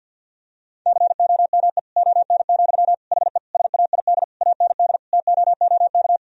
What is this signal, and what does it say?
Morse code, heard as a keyed tone, about 50 wpm, 700 Hz. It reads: COME ON, HE SAID AND TOOK
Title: Sentences from Top 1000 Words - Review Part 2 - 50wpm